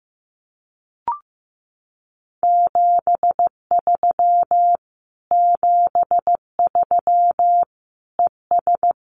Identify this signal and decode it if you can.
Morse code, heard as a keyed tone, about 14 wpm, 700 Hz. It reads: 73 73 ES
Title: Common QSO Phrases Encore 15wpm